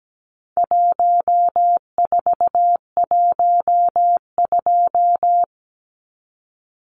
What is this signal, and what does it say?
Morse code, heard as a keyed tone, about 17 wpm, 700 Hz. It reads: 1412
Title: Numbers - 4-Digit 17wpm